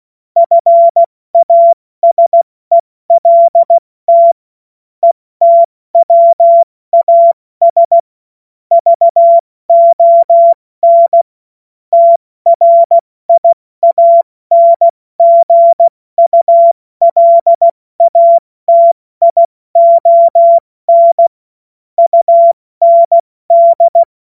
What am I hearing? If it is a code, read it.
Morse code, heard as a keyed tone, about 16 wpm, 680 Hz. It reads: FASELT ETWAS VON TRIANGULATION UND
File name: Morsekurs / Witze / Witz 16. Dezember 2025 70 F80